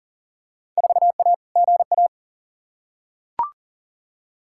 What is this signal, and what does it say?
Morse code, heard as a keyed tone, about 40 wpm, 700 Hz. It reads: 4A GA
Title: Field Day Exchanges - 40wpm